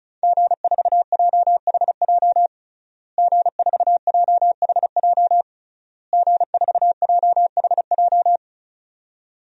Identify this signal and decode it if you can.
Morse code, heard as a keyed tone, about 36 wpm, 700 Hz. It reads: G4JHJ G4JHJ G4JHJ
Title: DXCC-100 Call Signs - 5 Characters - Repeated 3x - 35wpm